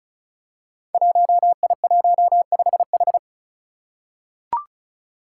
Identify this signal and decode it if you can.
Morse code, heard as a keyed tone, about 36 wpm, 700 Hz. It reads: 1I15H
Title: Five Challenging Letter-Number Combinations 35wpm